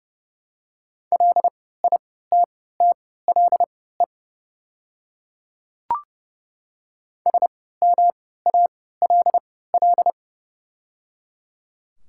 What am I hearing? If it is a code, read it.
Morse code, heard as a keyed tone, about 31 wpm, spaced as for 17 wpm, 700 Hz. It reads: LITTLE SMALL
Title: 3x Character Spacing - Top 200 Words - 30wpm